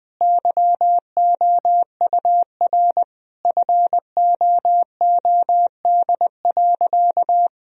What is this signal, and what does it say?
Morse code, heard as a keyed tone, about 21 wpm, 700 Hz. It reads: YOUR FOOD.
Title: Sentences from Top 1000 Words - Review Part 2 - 20wpm